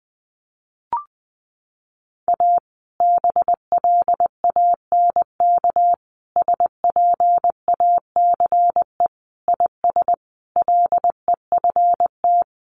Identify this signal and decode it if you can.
Morse code, heard as a keyed tone, about 20 wpm, 700 Hz. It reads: A BLANK SPACE IS LEFT
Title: Sentences from Top 1500 Words 20wpm